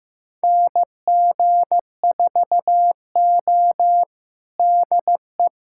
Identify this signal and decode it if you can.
Morse code, heard as a keyed tone, about 15 wpm, 700 Hz. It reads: NG4O DE
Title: USA Callsign DE Callsign - 4 Characters - 15wpm